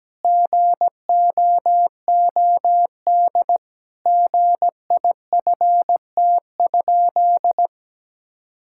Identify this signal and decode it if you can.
Morse code, heard as a keyed tone, about 17 wpm, 700 Hz. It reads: GOOD GIFT?
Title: Sentences from Top 1000 Words - Review Part 6 - 17wpm